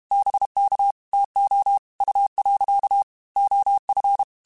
Morse code, heard as a keyed tone, about 32 wpm, 800 Hz. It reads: BK TO U. OF
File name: QSOs / QSOs-mp3s / WA7QOX 1@30